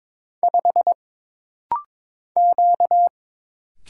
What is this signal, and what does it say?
Morse code, heard as a keyed tone, about 22 wpm, 700 Hz. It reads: 5 Q
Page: This rapid-fire timing is geared to push your ICR (Instant Character Recognition) skills.